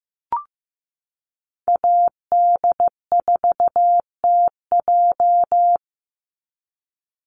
Morse code, heard as a keyed tone, about 15 wpm, 700 Hz. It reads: AD4TJ
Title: Call Signs Repeated 3x - 15wpm